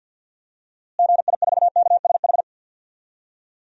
Morse code, heard as a keyed tone, about 53 wpm, 700 Hz. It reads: GI4KSH
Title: DXCC-100 Call Signs - 6 Characters - Repeated 2x - 50wpm